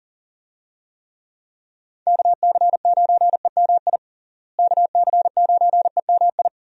Morse code, heard as a keyed tone, about 40 wpm, 700 Hz. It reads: KC9EMI KC9EMI
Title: USA Call Signs - 6 Characters - Repeated 2x - 40wpm